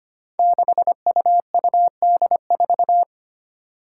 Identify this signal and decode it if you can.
Morse code, heard as a keyed tone, about 25 wpm, 700 Hz. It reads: 6UUD4
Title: Five Challenging Letter-Number Combinations 25wpm